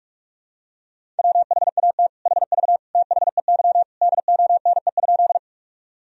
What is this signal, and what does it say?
Morse code, heard as a keyed tone, about 46 wpm, 700 Hz. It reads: WHAT HV THEY DONE?
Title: Sentences from Top 500 Words and Abbreviations 45wpm